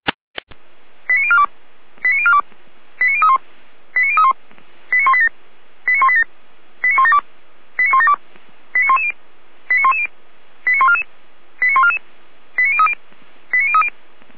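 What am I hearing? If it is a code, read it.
Five-tone selective call (ZVEI), heard as quick runs of five tones, tones of about 70 ms. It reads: 89042 89042 89031 89031 89187 89187 89172 89172 89110 89110 89144 89144 89030 89030
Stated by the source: ZVEI five-tone selective call 89166, received on SDR.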